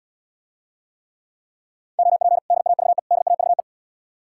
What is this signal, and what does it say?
Morse code, heard as a keyed tone, about 72 wpm, 700 Hz. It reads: OP DAVE DAVE